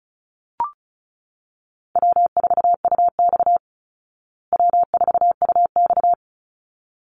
Morse code, heard as a keyed tone, about 34 wpm, 700 Hz. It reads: W4UX W4UX
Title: Easy Call Signs Repeated 2x - 35wpm